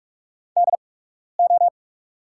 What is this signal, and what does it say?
Morse code, heard as a keyed tone, about 44 wpm, spaced as for 12 wpm, 700 Hz. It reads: DO
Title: 8x Character Spacing - Top 200 Words - 45wpm